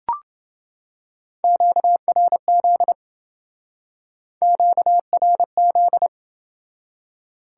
Morse code, heard as a keyed tone, about 28 wpm, 700 Hz. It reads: QRZ QRZ